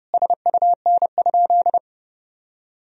Morse code, heard as a keyed tone, about 32 wpm, 700 Hz. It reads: SUN?
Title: Sentences from Top 1000 Words - Review Part 1 - 30wpm